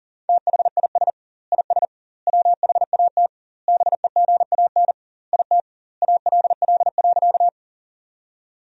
Morse code, heard as a keyed tone, about 40 wpm, 700 Hz. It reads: THIS IS WHAT BEGAN IT ALL.
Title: 2x Word-Spacing - Sentences from Top 300 Words - 40wpm